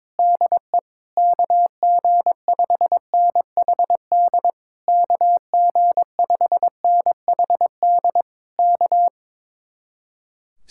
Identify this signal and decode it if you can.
Morse code, heard as a keyed tone, about 22 wpm, 700 Hz. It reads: DE KG5NHD KG5NHD K
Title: Speed-Racing Common QSO Phrases - 22wpm